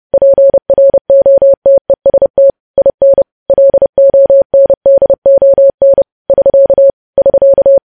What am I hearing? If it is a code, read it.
Morse code, heard as a keyed tone, about 30 wpm, 550 Hz. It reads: PROTEST IN LONDON <SK> <SK>
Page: news headlines in morse code at 30 wpm